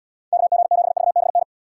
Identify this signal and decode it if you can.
Morse code, heard as a keyed tone, about 77 wpm, 700 Hz. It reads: KG7VBN